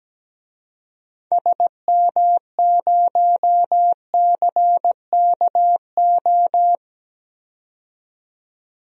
Morse code, heard as a keyed tone, about 17 wpm, 700 Hz. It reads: SM0CKO
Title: DXCC-100 Call Signs - 6 Characters - 17wpm